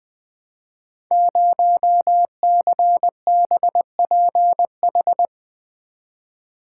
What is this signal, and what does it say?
Morse code, heard as a keyed tone, about 20 wpm, 700 Hz. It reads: 0CBPH
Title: Code Groups - 5 Letter:Numbers - 20wpm